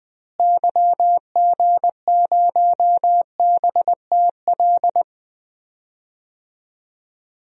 Morse code, heard as a keyed tone, about 20 wpm, 700 Hz. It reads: YG0BTL
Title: DXCC-100 Call Signs - 6 Characters - Repeated 2x - 20wpm